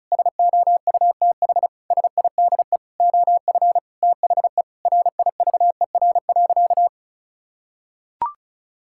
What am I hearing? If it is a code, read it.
Morse code, heard as a keyed tone, about 34 wpm, 700 Hz. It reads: SOUTH SIDE OF THE RIVER.
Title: Sentences from Top 1000 Words - Review Part 5 - 35wpm